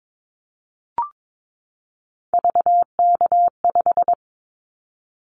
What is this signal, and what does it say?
Morse code, heard as a keyed tone, about 22 wpm, 700 Hz. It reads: VK5
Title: Callsign Prefixes for Top 100 DXCC - 22wpm